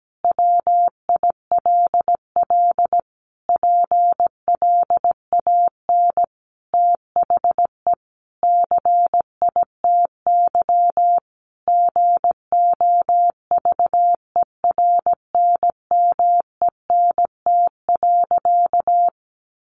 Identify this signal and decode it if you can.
Morse code, heard as a keyed tone, about 17 wpm, 700 Hz. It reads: WILL PLAN THE CITY GOVERNMENT.
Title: Sentences from Top 1400 Words 17wpm